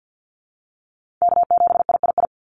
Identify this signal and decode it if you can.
Morse code, heard as a keyed tone, about 50 wpm, 700 Hz. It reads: K7III
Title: NA SOTA Chaser Callsigns 50wpm